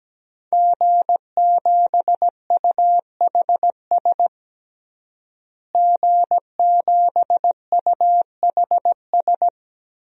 Morse code, heard as a keyed tone, about 17 wpm, 700 Hz. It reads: G7UHS G7UHS
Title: DXCC-100 Call Signs - 5 Characters - Repeated 3x - 17wpm